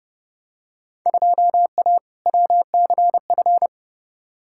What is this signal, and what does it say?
Morse code, heard as a keyed tone, about 30 wpm, 700 Hz. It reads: 2A WCF
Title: Field Day Exchanges Encore - 30wpm